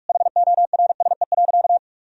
Morse code, heard as a keyed tone, about 46 wpm, 700 Hz. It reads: HORSE.
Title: Sentences from Top 1000 Words - Review Part 4 - 45wpm